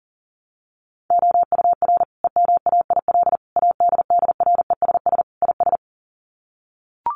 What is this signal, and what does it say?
Morse code, heard as a keyed tone, about 40 wpm, 700 Hz. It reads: OUR EMAIL ADDRESS IS
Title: Sets of 5 Words Encore2 40wpm